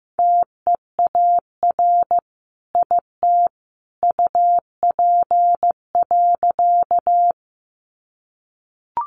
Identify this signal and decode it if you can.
Morse code, heard as a keyed tone, about 15 wpm, 700 Hz. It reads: TEAR IT UP.
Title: A Sound Of Thunder 15wpm